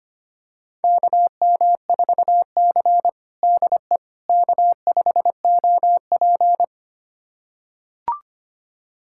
Morse code, heard as a keyed tone, about 25 wpm, 700 Hz. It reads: KM4C DE K5OP
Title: USA Callsign DE Callsign - 4 Characters - 25wpm